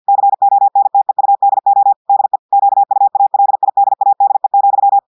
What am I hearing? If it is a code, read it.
Morse code, heard as a keyed tone, about 50 wpm, 800 Hz. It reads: CONTEUDO DE QUALIDADE,
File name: portugais texte en clair 2 vitesse 50 mots minutes